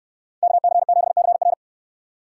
Morse code, heard as a keyed tone, about 65 wpm, 700 Hz. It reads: KC7CR